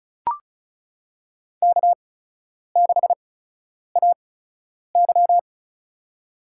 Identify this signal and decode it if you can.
Morse code, heard as a keyed tone, about 36 wpm, spaced as for 10 wpm, 700 Hz. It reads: K6AY
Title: USA Call Signs - 4 Characters - 8x Character Spacing - 35wpm